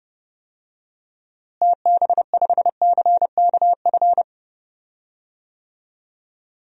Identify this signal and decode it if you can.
Morse code, heard as a keyed tone, about 32 wpm, 700 Hz. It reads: TB5CKF
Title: DXCC-100 Call Signs - 6 Characters - Repeated 2x - 30wpm